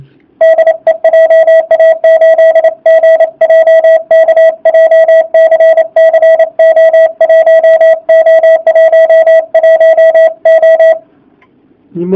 Morse code, heard as a keyed tone, about 28 wpm, 650 Hz. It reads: DE JA8GJK JCC O1O11O